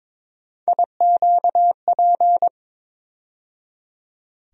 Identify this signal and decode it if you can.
Morse code, heard as a keyed tone, about 22 wpm, 700 Hz. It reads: IQP
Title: USA Call Sign Postfixes - 3 Letters 22wpm